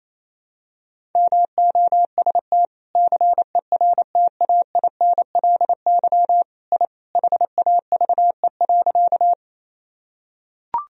Morse code, heard as a keyed tone, about 28 wpm, 700 Hz. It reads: MOST CERTAINLY I HAVE.
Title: Sentences from Top 1000 Words - Review Part 1 - 28wpm